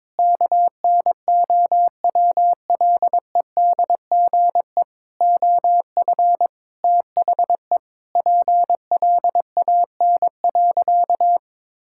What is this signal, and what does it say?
Morse code, heard as a keyed tone, about 22 wpm, 700 Hz. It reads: KNOWLEDGE OF THE PLAN.
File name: practice / 22wpm / Sentences with 8-Word Phrases - 22wpm